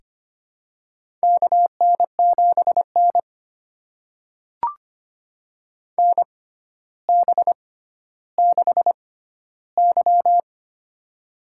Morse code, heard as a keyed tone, about 25 wpm, spaced as for 13 wpm, 700 Hz. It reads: KN7N NB6Y
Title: USA Call Signs - 6x Character Spacing - 25wpm